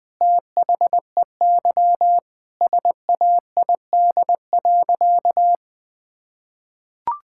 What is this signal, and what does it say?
Morse code, heard as a keyed tone, about 20 wpm, 700 Hz. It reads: THEY SAID.
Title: Sentences with 2-Word Phrases - Spoken Beginning - 20wpm